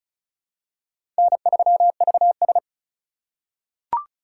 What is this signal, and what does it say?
Morse code, heard as a keyed tone, about 36 wpm, 690 Hz. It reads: N3VS
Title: USA Call Signs - 4 Characters - 2x Character Spacing - 35wpm